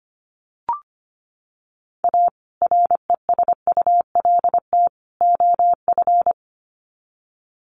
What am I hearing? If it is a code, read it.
Morse code, heard as a keyed tone, about 25 wpm, 700 Hz. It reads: A RESULT OF
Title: Sets of 3 Words 25wpm